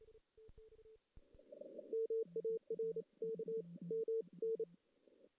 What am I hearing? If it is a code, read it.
Morse code, heard as a keyed tone, about 28 wpm, 410 Hz. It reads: MARK M N
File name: 2016 JAN CW NAQP / 2016 Jan naqp k5ot - 0-2